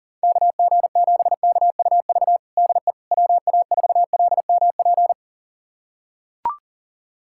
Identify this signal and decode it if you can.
Morse code, heard as a keyed tone, about 40 wpm, 700 Hz. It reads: KG7KUV DE WA4LMP
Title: USA Callsign DE Callsign - 6 Characters - 40wpm